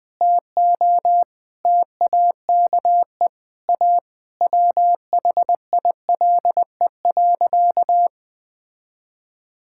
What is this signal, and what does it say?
Morse code, heard as a keyed tone, about 20 wpm, 700 Hz. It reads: TO TAKE A WHILE.
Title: Sentences with 4-Word Phrases - Spoken Beginning - 20wpm